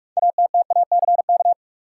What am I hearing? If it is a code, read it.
Morse code, heard as a keyed tone, about 44 wpm, 700 Hz. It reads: ATTACK
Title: Sentences from Top 1500 Words - New Word Drill - 45wpm